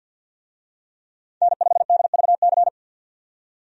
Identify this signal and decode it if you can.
Morse code, heard as a keyed tone, about 50 wpm, 700 Hz. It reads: N5DVC
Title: USA Call Signs - 5 Characters - 50wpm